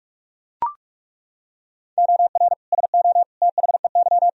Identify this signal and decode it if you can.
Morse code, heard as a keyed tone, about 42 wpm, 700 Hz. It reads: OR SO THEY
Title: Sentences from Top 100 Words and Abbreviations 45wpm